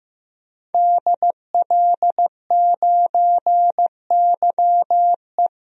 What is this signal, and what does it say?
Morse code, heard as a keyed tone, about 16 wpm, 700 Hz. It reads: DL9YE
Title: DXCC-100 Call Signs - 5 Characters - 15wpm